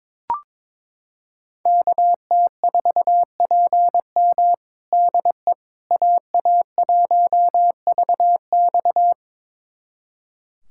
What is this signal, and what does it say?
Morse code, heard as a keyed tone, about 21 wpm, 700 Hz. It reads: KT4PM DE AA1VX
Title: USA Callsign DE Callsign - 5 Characters - 22wpm